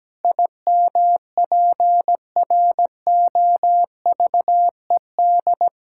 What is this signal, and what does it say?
Morse code, heard as a keyed tone, about 17 wpm, 700 Hz. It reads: IMPROVED
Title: Pangram Challenge v2 - 17wpm